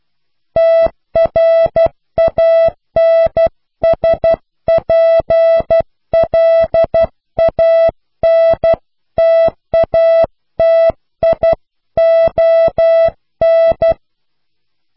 Morse code, heard as a keyed tone, about 12 wpm, 660 Hz. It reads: TRANSPLANTATION
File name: cw / Pseudo woorden (15)